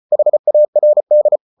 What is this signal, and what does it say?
Morse code, heard as a keyed tone, about 34 wpm, 600 Hz. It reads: HARD